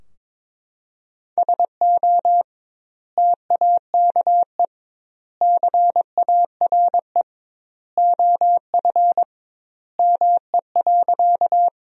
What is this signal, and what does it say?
Morse code, heard as a keyed tone, about 21 wpm, 700 Hz. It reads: SO TAKE CARE OF ME.
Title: 2x Word-Spacing - Sentences from Top 300 Words - 22wpm